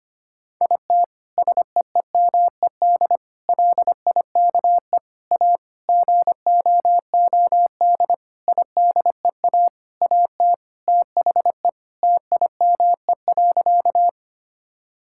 Morse code, heard as a keyed tone, about 25 wpm, 700 Hz. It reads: IT SEEMED LIKE A GOOD IDEA AT THE TIME.
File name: practice / 25wpm / Sentences with 9-Word Phrases - 25wpm